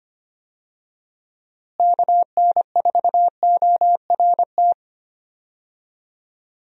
Morse code, heard as a keyed tone, about 25 wpm, 700 Hz. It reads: KN4ORT
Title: USA Call Signs - 6 Characters - Repeated 2x - 25wpm